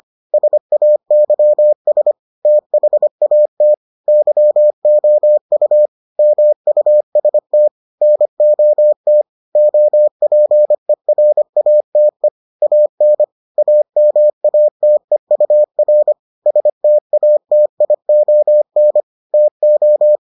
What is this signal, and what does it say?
Morse code, heard as a keyed tone, about 25 wpm, 600 Hz. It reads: SAYS THAT YOU MUST NOT OPERATE AN AMATEUR STATION TO